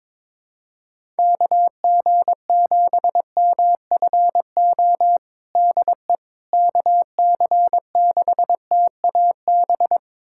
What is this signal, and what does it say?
Morse code, heard as a keyed tone, about 22 wpm, 700 Hz. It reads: KG7MFO DE KC6TAB
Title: USA Callsign DE Callsign 22wpm